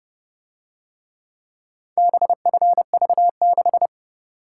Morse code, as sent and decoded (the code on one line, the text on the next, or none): -... ..-. ...- -....
BFV6